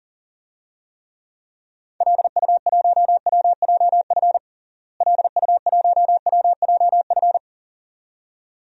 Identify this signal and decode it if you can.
Morse code, heard as a keyed tone, about 40 wpm, 700 Hz. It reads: LU1WJF LU1WJF
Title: DXCC-100 Call Signs - 6 Characters - Repeated 2x - 40wpm